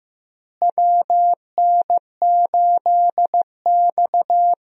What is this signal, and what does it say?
Morse code, heard as a keyed tone, about 15 wpm, 700 Hz. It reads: WN8X
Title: USA Call Signs - 4 Characters - Repeated 3x - 15wpm